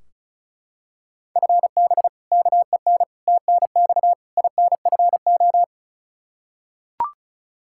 Morse code, heard as a keyed tone, about 36 wpm, 700 Hz. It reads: FB KEN TNX INFO
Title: Practice QSO - Exchange 3 - More Info - 35wpm